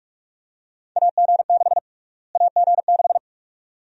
Morse code, heard as a keyed tone, about 44 wpm, 700 Hz. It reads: AG6 AG6
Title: USA Call Sign Prefixes - Repeated 2x - 45wpm